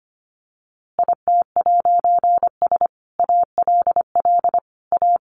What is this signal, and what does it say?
Morse code, heard as a keyed tone, about 25 wpm, 700 Hz. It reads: IT'S ALL A
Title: A Sound Of Thunder 25wpm